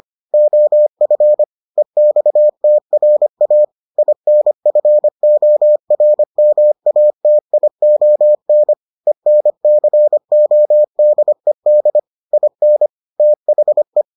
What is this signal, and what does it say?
Morse code, heard as a keyed tone, about 25 wpm, 600 Hz. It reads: OF EXTRA INFORMATION ENCODED IN THE